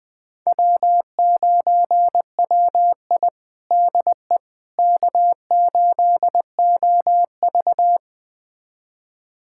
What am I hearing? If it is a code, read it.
Morse code, heard as a keyed tone, about 20 wpm, 700 Hz. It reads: W9WI DE K8OV
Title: USA Callsign DE Callsign - 4 Characters - 20wpm